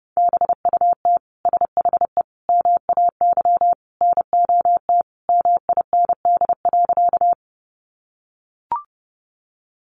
Morse code, heard as a keyed tone, about 30 wpm, 700 Hz. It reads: BUT SHE MAY NOT MIND.
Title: Sentences from Top 600 Words 30wpm